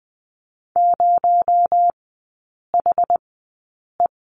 Morse code, heard as a keyed tone, about 20 wpm, spaced as for 8 wpm, 700 Hz. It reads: 0HE
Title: 218 Lesson 35e - Number 0 - Three Characters 20wpm